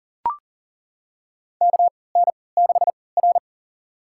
Morse code, heard as a keyed tone, about 40 wpm, spaced as for 23 wpm, 700 Hz. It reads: KN6R
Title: USA Call Signs - 4 Characters - 3x Character Spacing - 40wpm